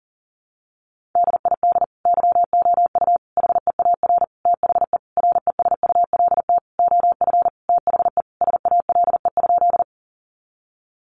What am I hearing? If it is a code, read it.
Morse code, heard as a keyed tone, about 37 wpm, 700 Hz. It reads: DID YOU HEAR THE RESULT OF THE SALE?
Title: Sentences from Top 1400 Words 40wpm